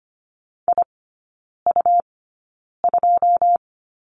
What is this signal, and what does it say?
Morse code, heard as a keyed tone, about 25 wpm, spaced as for 9 wpm, 700 Hz. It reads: IU2
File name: Lesson 17e - Letter u - Three Characters 25wpm